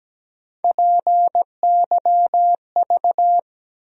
Morse code, heard as a keyed tone, about 17 wpm, 700 Hz. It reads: PYV
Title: USA Call Sign Postfixes - 3 Letters 17wpm